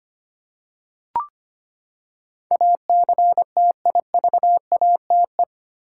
Morse code, heard as a keyed tone, about 25 wpm, 700 Hz. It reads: ACTIVATE
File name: Eight Letter Words - 25wpm